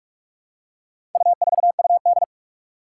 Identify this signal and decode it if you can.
Morse code, heard as a keyed tone, about 44 wpm, 700 Hz. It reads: U4UD
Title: Four Challenging Letter-Number Combinations 45wpm